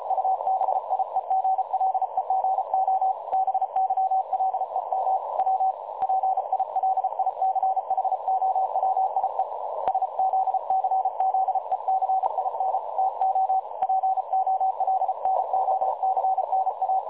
Morse code, heard as a keyed tone, about 33 wpm, 760 Hz, with noise. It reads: UBIEGLEGO BYLEM W PILICHOWIE SZCZECIN TO POLNOCNA
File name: dlazaawansowanych / nagraniazpasma / websdr recording 3565.0kHz